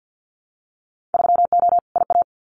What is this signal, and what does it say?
Morse code, heard as a keyed tone, about 50 wpm, 700 Hz. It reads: 3O IA
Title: Winter Field Day Exchanges 50wpm